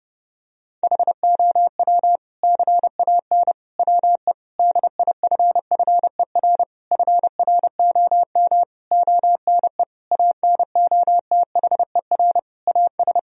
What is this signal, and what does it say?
Morse code, heard as a keyed tone, about 32 wpm, 700 Hz. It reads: HOW CAN WE DIFFER FROM ONE ANOTHER AS
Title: Sentences from Top 1000 Words - Review Part 1 - 30wpm